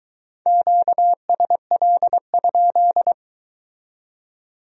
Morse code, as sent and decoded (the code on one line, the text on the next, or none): --.- ... .-.. ..--..
QSL?